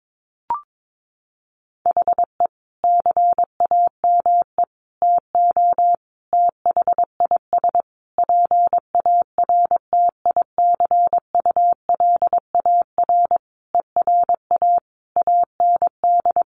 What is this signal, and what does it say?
Morse code, heard as a keyed tone, about 22 wpm, 700 Hz. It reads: HE CAME TO THIS PARTICULAR ERA AND
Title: A Sound Of Thunder 22wpm